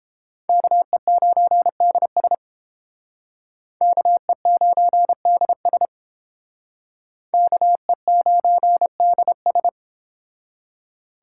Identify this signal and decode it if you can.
Morse code, heard as a keyed tone, about 30 wpm, 700 Hz. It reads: KE9DS KE9DS KE9DS